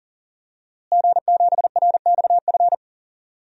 Morse code, heard as a keyed tone, about 40 wpm, 700 Hz. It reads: G7RXF
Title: DXCC-100 Call Signs - 5 Characters - Repeated 2x - 40wpm